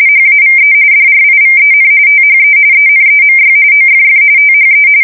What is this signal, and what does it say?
RTTY, about 45.45 baud. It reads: HAVE DRIVEN THE ENEMY OFF THE
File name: fsk170 / battleshipnj